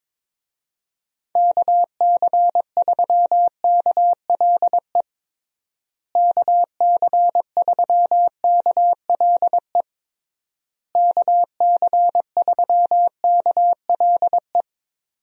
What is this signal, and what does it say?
Morse code, heard as a keyed tone, about 22 wpm, 700 Hz. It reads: KC3KLE KC3KLE KC3KLE
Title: USA Call Signs - 6 Characters - Repeated 3x - 22wpm